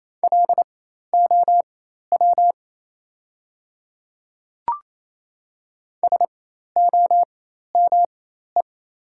Morse code, heard as a keyed tone, about 27 wpm, spaced as for 13 wpm, 690 Hz. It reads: LOW SOME
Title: 4x Character Spacing - Top 200 Words - 28wpm